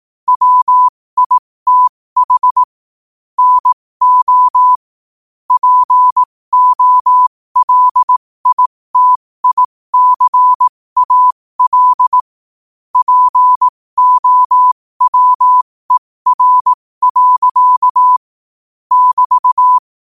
Morse code, heard as a keyed tone, about 18 wpm, spaced as for 15 wpm, 1000 Hz. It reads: WITH NO POLITICAL POWER. <BT>
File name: qotdfiles / 15wpm